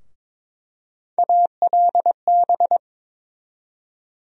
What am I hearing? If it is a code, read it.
Morse code, heard as a keyed tone, about 22 wpm, 700 Hz. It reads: ALB
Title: USA Call Sign Postfixes - 3 Letters 22wpm